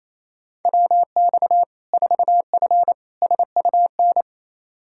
Morse code, heard as a keyed tone, about 27 wpm, 700 Hz. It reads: WX 4F SUN
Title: Practice QSO - Exchange 2 - Info - 28wpm